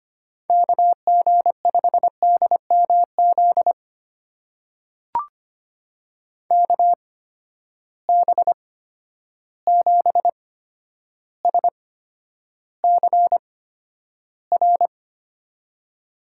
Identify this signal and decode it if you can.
Morse code, heard as a keyed tone, about 25 wpm, spaced as for 11 wpm, 700 Hz. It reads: KG5DMZ KB7SCR
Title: USA Call Signs - 6 Characters - 8x Character Spacing - 25wpm